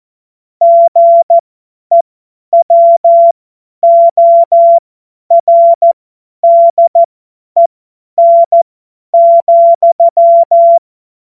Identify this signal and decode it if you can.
Morse code, heard as a keyed tone, about 14 wpm, spaced as for 10 wpm, 680 Hz. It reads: GEWORDEN,
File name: Morsekurs / Witze / Witz 29. Juni 2021, 50 F70